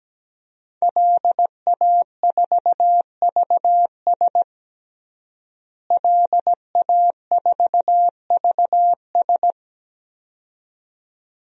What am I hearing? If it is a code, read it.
Morse code, heard as a keyed tone, about 17 wpm, 700 Hz. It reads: LA4VS LA4VS
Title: DXCC-100 Call Signs - 5 Characters - Repeated 2x - 17wpm